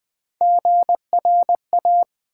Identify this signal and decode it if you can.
Morse code, heard as a keyed tone, about 20 wpm, 700 Hz. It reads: GRA
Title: Build A Word - 1 Letter At A Time - 5-Letter Words - 20wpm